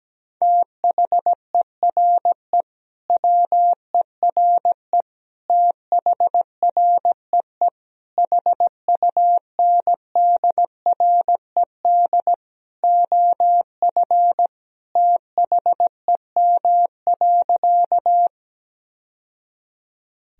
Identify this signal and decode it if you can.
Morse code, heard as a keyed tone, about 17 wpm, 700 Hz. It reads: THERE WERE THREE HUNDRED OF THEM.
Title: Sentences from Top 1000 Words - Review Part 2 - 17wpm